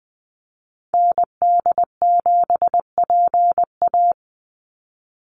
Morse code, heard as a keyed tone, about 20 wpm, 700 Hz. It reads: ND7PA
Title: NA SOTA Activator Callsigns 20wpm